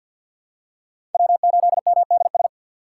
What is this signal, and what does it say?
Morse code, heard as a keyed tone, about 53 wpm, 700 Hz. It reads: W8MDS
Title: USA Call Signs - 5 Characters - 3x Character Spacing - 50wpm